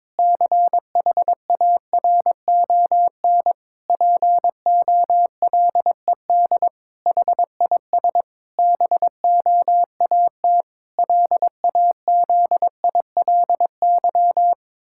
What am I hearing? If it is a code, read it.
Morse code, heard as a keyed tone, about 22 wpm, 700 Hz. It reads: CHARON POLED HIS BOAT LAZILY
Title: Pangram Challenge v2 - 22wpm